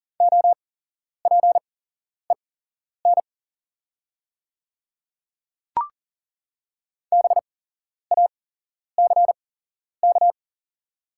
Morse code, heard as a keyed tone, about 38 wpm, spaced as for 11 wpm, 690 Hz. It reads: OPEN BACK
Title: 8x Character Spacing - Top 300 Words - 40wpm